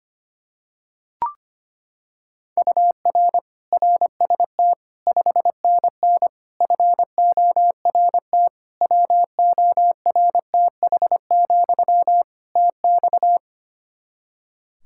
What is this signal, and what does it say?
Morse code, heard as a keyed tone, about 25 wpm, 700 Hz. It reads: UR RST 5NN FORT WORTH, TX
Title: Practice QSO - Exchange 1 - Beginning - 25wpm